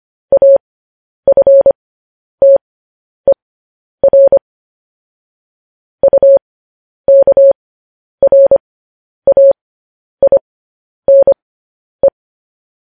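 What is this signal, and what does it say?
Morse code, heard as a keyed tone, about 25 wpm, spaced as for 10 wpm, 550 Hz. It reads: AFTER UKRAINE
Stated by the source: Morse transmission of news headlines.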